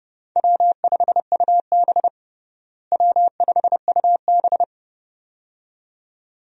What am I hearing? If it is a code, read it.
Morse code, heard as a keyed tone, about 31 wpm, 700 Hz. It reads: W5UB W5UB
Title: USA Call Signs - 4 Characters - Repeated 2x - 30wpm